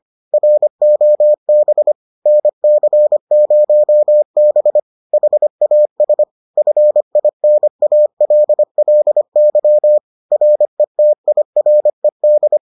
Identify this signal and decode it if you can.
Morse code, heard as a keyed tone, about 25 wpm, 600 Hz. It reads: ROB NC0B HAS FINALLY RETIRED